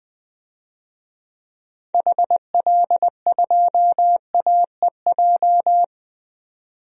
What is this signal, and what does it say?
Morse code, heard as a keyed tone, about 21 wpm, 700 Hz. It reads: HL2AEJ
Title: DXCC-100 Call Signs - 6 Characters - 20wpm